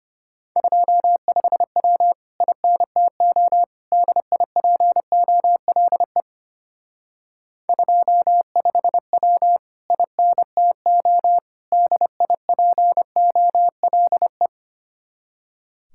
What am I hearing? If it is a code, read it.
Morse code, heard as a keyed tone, about 32 wpm, 700 Hz. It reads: 25W INTO DIPOLE 25W INTO DIPOLE